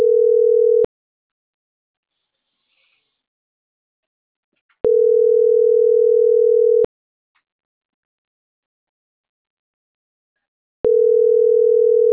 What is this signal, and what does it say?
Ringback tone (North America), 440 + 480 Hz.